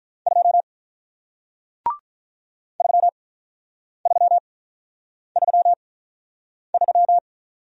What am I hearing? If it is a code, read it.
Morse code, heard as a keyed tone, about 47 wpm, spaced as for 9 wpm, 700 Hz. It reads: 2 3333